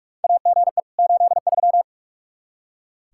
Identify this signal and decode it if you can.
Morse code, heard as a keyed tone, about 42 wpm, 700 Hz. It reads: AGE 83
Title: Practice QSO - Exchange 3 - More Info - 45wpm